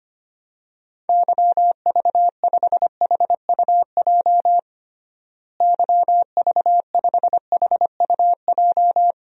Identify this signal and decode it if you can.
Morse code, heard as a keyed tone, about 25 wpm, 700 Hz. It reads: YV5HUJ YV5HUJ
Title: DXCC-100 Call Signs - 6 Characters - Repeated 2x - 25wpm